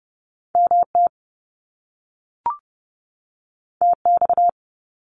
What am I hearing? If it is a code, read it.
Morse code, heard as a keyed tone, about 30 wpm, 700 Hz. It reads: MT TX
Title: US States 30wpm